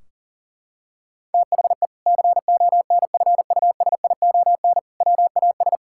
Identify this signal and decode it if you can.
Morse code, heard as a keyed tone, about 40 wpm, 700 Hz. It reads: THE CONFUSION WAS
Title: Sentences from Top 1600 Words 40wpm